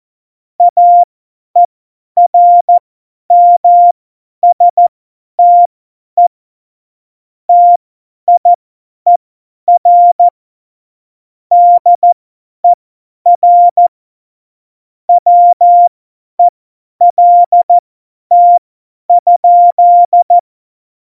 Morse code, heard as a keyed tone, about 14 wpm, spaced as for 10 wpm, 700 Hz. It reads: AERMSTE TIER DER WELT?
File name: Morsekurs / Witze / Witz 28 April 2020, 50 F70